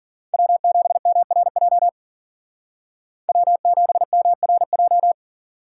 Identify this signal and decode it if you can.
Morse code, heard as a keyed tone, about 52 wpm, 700 Hz. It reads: W7MRJ W7MRJ